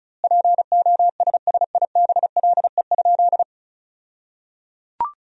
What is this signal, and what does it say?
Morse code, heard as a keyed tone, about 36 wpm, 700 Hz. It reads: POSSIBLE?
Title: 2x Word-Spacing - Sentences from Top 500 Words - 35wpm